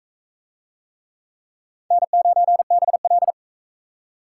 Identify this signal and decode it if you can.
Morse code, heard as a keyed tone, about 42 wpm, 700 Hz. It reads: N9BL